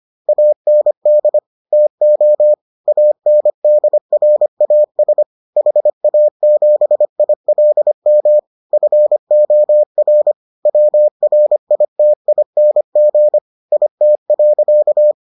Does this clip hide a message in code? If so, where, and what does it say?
Yes: Morse code from 0.3 to 15.1 s, heard as a keyed tone, about 25 wpm, 600 Hz. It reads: AND TO ANDRAS HA7ILM FOR WRITING IT.